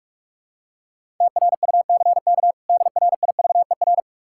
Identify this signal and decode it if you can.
Morse code, heard as a keyed tone, about 46 wpm, 700 Hz. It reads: TRUCK DRIVER
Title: K1USN SST QSO Practice 45wpm